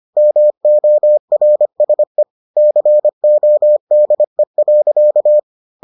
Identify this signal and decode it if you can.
Morse code, heard as a keyed tone, about 25 wpm, 600 Hz. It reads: MORSE CODE.